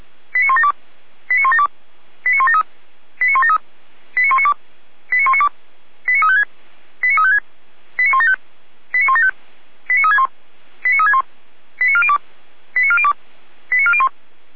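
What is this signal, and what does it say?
Five-tone selective call (ZVEI), heard as quick runs of five tones, tones of about 70 ms. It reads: 89172 89172 89173 89173 89192 89192 89356 89356 89175 89175 89361 89361 89402 89402 89401
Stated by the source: Received on SDR; ZVEI five-tone call 89350, channel 455.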